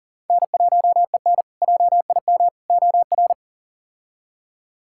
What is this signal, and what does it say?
Morse code, heard as a keyed tone, about 40 wpm, 700 Hz. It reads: N1EN JIM OR
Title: K1USN SST QSO Practice - 40wpm